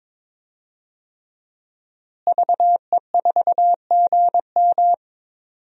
Morse code, heard as a keyed tone, about 22 wpm, 700 Hz. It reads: VE4GM
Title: DXCC-100 Call Signs - 5 Characters - Repeated 3x - 22wpm